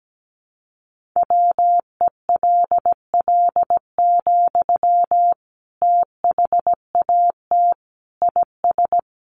Morse code, heard as a keyed tone, about 17 wpm, 700 Hz. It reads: WELL, THAT IS
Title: Sentences from Top 900 Words 17wpm